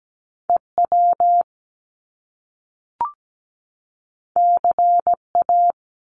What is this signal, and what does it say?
Morse code, heard as a keyed tone, about 17 wpm, 700 Hz. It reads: EW CA
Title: Top Two Letter Combinations in Words 17wpm